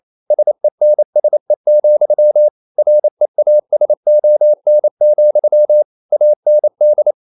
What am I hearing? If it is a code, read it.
Morse code, heard as a keyed tone, about 28 wpm, 600 Hz. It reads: SENSE, REASON, AND
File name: CW / Practice / EnglishQuotations0128